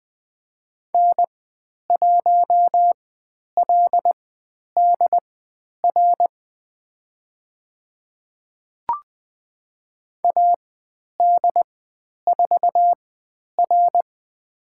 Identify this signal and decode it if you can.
Morse code, heard as a keyed tone, about 20 wpm, spaced as for 10 wpm, 700 Hz. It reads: N1LDR AD4R
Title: 102 Lesson 17g - Letter u - Call Signs 20wpm